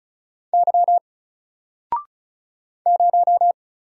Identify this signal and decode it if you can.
Morse code, heard as a keyed tone, about 34 wpm, 700 Hz. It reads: Y 0
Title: Single Character (Letter:Number:Punctuation:Prosign) 35wpm